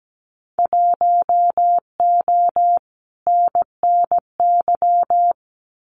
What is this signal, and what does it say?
Morse code, heard as a keyed tone, about 17 wpm, 700 Hz. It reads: 1O NNY
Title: Winter Field Day Exchanges 17wpm